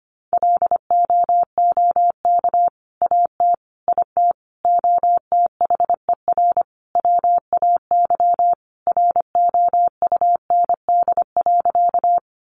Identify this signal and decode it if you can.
Morse code, heard as a keyed tone, about 25 wpm, 700 Hz. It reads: LOOK AT IT OTHER WAY ROUND.
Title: Sentences from Top 200 Words 25wpm